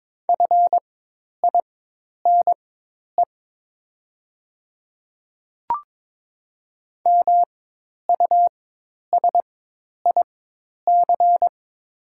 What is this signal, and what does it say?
Morse code, heard as a keyed tone, about 21 wpm, spaced as for 10 wpm, 700 Hz. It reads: FINE MUSIC
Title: 4x Character Spacing - Top 500 Words - 22wpm